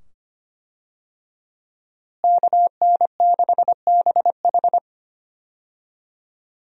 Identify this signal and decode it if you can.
Morse code, heard as a keyed tone, about 25 wpm, 700 Hz. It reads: KN6BH
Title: USA Call Signs - 5 Characters - Repeated 2x - 25wpm